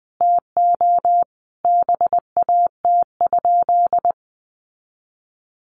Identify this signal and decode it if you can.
Morse code, heard as a keyed tone, about 20 wpm, 700 Hz. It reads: TO BAT?
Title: Sentences from Top 1500 Words 20wpm